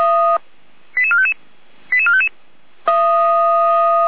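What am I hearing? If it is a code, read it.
Five-tone selective call (ZVEI), heard as quick runs of five tones, tones of about 70 ms. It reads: 88455 88455
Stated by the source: Five-tone ZVEI selective call 88352, received on SDR.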